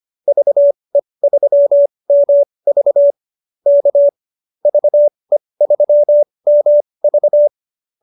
Morse code, heard as a keyed tone, about 25 wpm, spaced as for 20 wpm, 590 Hz. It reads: VE3MV K VE3MV
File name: QSO220 20